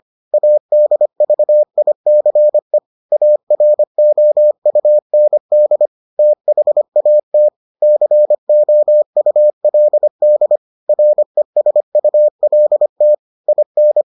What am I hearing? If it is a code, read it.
Morse code, heard as a keyed tone, about 25 wpm, 600 Hz. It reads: ADVICE AROUND THAT COULD RESULT IN